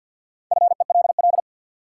Morse code, heard as a keyed tone, about 53 wpm, 700 Hz. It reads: FELL
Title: Sentences from Top 1600 Words - New Word Drill - 50wpm